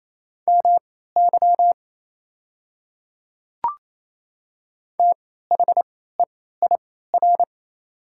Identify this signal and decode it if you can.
Morse code, heard as a keyed tone, about 28 wpm, spaced as for 16 wpm, 690 Hz. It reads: MY THEIR
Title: 3x Character Spacing - Top 100 Words - 28wpm